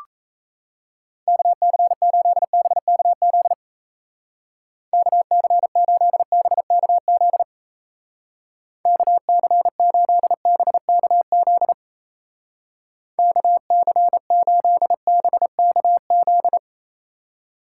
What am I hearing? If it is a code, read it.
Morse code, heard as a keyed tone, about 36 wpm, 700 Hz. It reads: KC8BKZ KC8BKZ KC8BKZ KC8BKZ